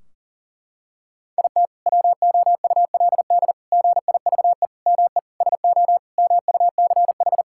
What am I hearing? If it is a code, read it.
Morse code, heard as a keyed tone, about 40 wpm, 700 Hz. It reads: IT WOULD GIVE ME SO MUCH
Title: Sentences from Top 1600 Words 40wpm